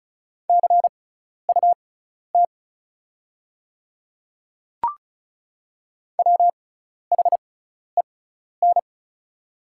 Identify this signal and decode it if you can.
Morse code, heard as a keyed tone, about 34 wpm, spaced as for 12 wpm, 690 Hz. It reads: CUT WHEN
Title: 6x Character Spacing - Top 500 Words - 35wpm